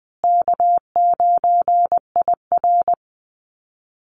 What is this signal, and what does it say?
Morse code, heard as a keyed tone, about 20 wpm, 700 Hz. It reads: K9IR
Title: Easy Call Signs Repeated 2x - 20wpm